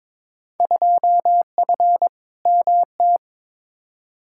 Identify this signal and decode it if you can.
Morse code, heard as a keyed tone, about 22 wpm, 700 Hz. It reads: 2F MT
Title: Field Day Exchanges Encore - 22wpm